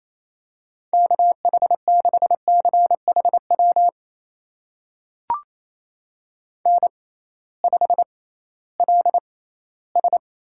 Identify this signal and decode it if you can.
Morse code, heard as a keyed tone, about 28 wpm, spaced as for 16 wpm, 700 Hz. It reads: KH6CHW N5LS
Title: USA Call Signs - 6x Character Spacing - 28wpm